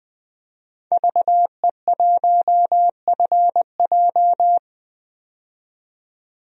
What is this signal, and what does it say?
Morse code, heard as a keyed tone, about 20 wpm, 700 Hz. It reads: VE1FJ
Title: DXCC-100 Call Signs - 5 Characters - 20wpm